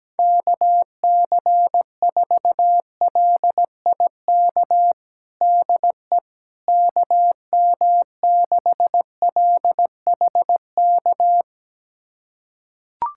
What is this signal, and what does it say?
Morse code, heard as a keyed tone, about 17 wpm, 700 Hz. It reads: KC4LIK DE KM6LHK
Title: USA Callsign DE Callsign - 6 Characters - 17wpm